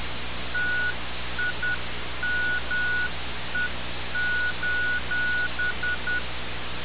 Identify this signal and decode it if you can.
Morse code, heard as a keyed tone, about 10 wpm, 1460 Hz. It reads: TIME:
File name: radiocap / REC z